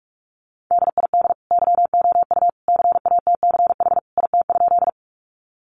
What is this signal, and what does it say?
Morse code, heard as a keyed tone, about 44 wpm, 700 Hz. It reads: DID YOU CATCH IT?
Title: Sentences from Top 1500 Words 45wpm